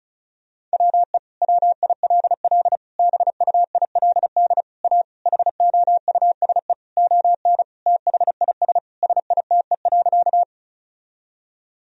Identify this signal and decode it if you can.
Morse code, heard as a keyed tone, about 36 wpm, 700 Hz. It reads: WE WILL BUILD A HOUSE ON THIS SITE.
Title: Sentences from Top 1000 Words - Review Part 4 - 35wpm